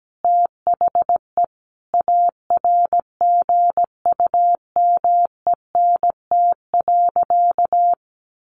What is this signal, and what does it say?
Morse code, heard as a keyed tone, about 17 wpm, 700 Hz. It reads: THE ARGUMENT.
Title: Sentences from Top 1400 Words 17wpm